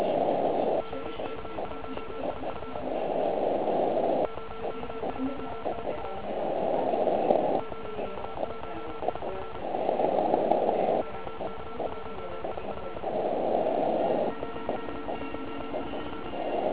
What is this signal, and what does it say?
Morse code, heard as a keyed tone, about 35 wpm, 1340 Hz, with noise. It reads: OK1IF OK1IF OK1IF OK1IF OK1IF
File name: Archiv / CUC129 / CUC129 pileup